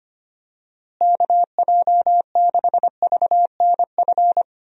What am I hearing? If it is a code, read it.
Morse code, heard as a keyed tone, about 25 wpm, 700 Hz. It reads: KJ6VNF
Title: USA Call Signs - 6 Characters - Repeated 3x - 25wpm